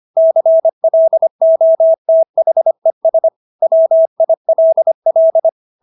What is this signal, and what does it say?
Morse code, heard as a keyed tone, about 25 wpm, 650 Hz. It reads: CLOTHES WILL